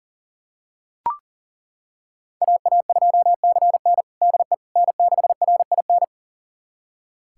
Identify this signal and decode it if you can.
Morse code, heard as a keyed tone, about 40 wpm, 700 Hz. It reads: AA2CN DE N6RIN
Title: USA Callsign DE Callsign - 5 Characters - 40wpm